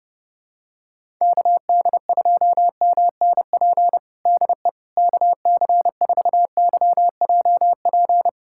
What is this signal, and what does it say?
Morse code, heard as a keyed tone, about 32 wpm, 700 Hz. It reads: KD2MNP DE KC4YJP
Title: USA Callsign DE Callsign 30wpm